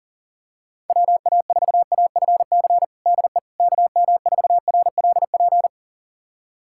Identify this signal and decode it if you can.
Morse code, heard as a keyed tone, about 40 wpm, 700 Hz. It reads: WA4AFC DE KM4RLP
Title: USA Callsign DE Callsign - 6 Characters - 40wpm